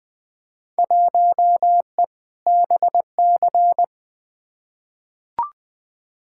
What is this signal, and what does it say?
Morse code, heard as a keyed tone, about 20 wpm, 700 Hz. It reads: 1E BC
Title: Field Day Exchanges - 20wpm